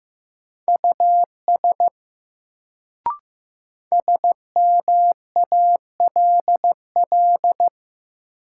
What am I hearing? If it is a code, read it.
Morse code, heard as a keyed tone, about 15 wpm, 700 Hz. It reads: US SMALL
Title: Sentences from Top 200 Words - New Word Drill - 15wpm